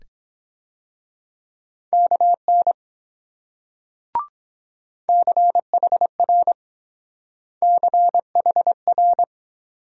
Morse code, heard as a keyed tone, about 22 wpm, 700 Hz. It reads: KN CHR CHR